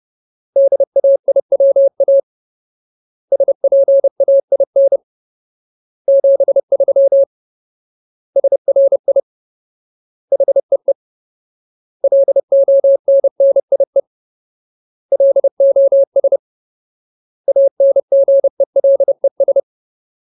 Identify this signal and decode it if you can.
Morse code, heard as a keyed tone, about 30 wpm, 550 Hz. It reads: DAIWA SPAIN 73 SRI HEE LONNIE LOS ANGELES
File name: Ultimate Word List / 30WPM Ultimate